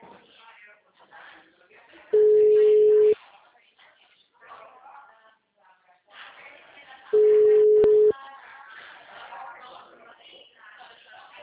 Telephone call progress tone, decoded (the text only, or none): ringback tone